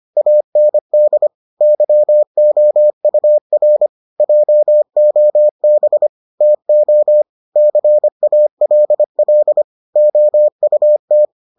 Morse code, heard as a keyed tone, about 25 wpm, 600 Hz. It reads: AND YOUR JOB TO CALL OUT